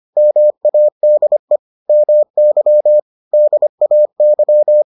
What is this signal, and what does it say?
Morse code, heard as a keyed tone, about 25 wpm, 600 Hz. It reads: MADE MY DAY